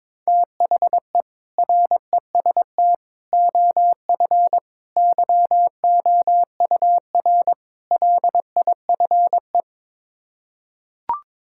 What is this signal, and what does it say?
Morse code, heard as a keyed tone, about 21 wpm, 700 Hz. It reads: THE REST OF YOUR LIFE
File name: practice / 22wpm / Build A Phrase - 5-Word Phrases - 22wpm